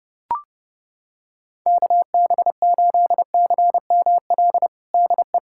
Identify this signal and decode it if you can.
Morse code, heard as a keyed tone, about 32 wpm, 700 Hz. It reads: KB8CML DE
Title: USA Callsign DE Callsign 30wpm